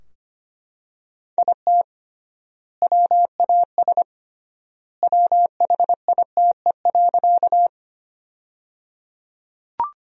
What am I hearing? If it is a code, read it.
Morse code, heard as a keyed tone, about 25 wpm, 700 Hz. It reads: IT WAS WHITE.
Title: 3x Word-Spacing - Sentences from Top 300 Words - 25wpm